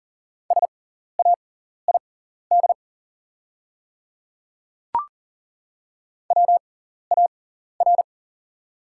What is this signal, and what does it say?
Morse code, heard as a keyed tone, about 40 wpm, spaced as for 14 wpm, 690 Hz. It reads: SAID WAR
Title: 6x Character Spacing - Top 500 Words - 40wpm